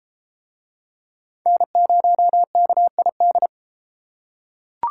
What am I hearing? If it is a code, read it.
Morse code, heard as a keyed tone, about 33 wpm, 700 Hz. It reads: N0KID